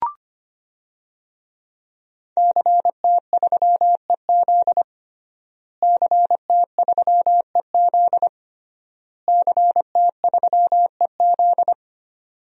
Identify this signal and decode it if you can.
Morse code, heard as a keyed tone, about 25 wpm, 700 Hz. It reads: CT3EZ CT3EZ CT3EZ
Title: DXCC-100 Call Signs - 5 Characters - Repeated 3x - 25wpm